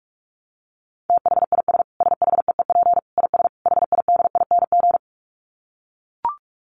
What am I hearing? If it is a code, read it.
Morse code, heard as a keyed tone, about 44 wpm, 700 Hz. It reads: THIS SHEEP IS HIDING
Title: Dit Challenge 45wpm